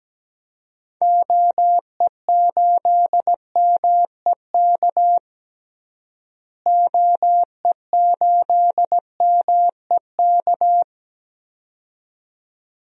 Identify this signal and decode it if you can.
Morse code, heard as a keyed tone, about 17 wpm, 700 Hz. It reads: OE8MEK OE8MEK
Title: DXCC-100 Call Signs - 6 Characters - Repeated 2x - 17wpm